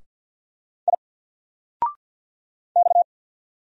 Morse code, heard as a keyed tone, about 50 wpm, 700 Hz. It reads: I X
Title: Single Letter-Number 50wpm